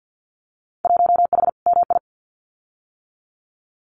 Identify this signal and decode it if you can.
Morse code, heard as a keyed tone, about 48 wpm, 700 Hz. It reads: 1H MI
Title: Winter Field Day Exchanges 50wpm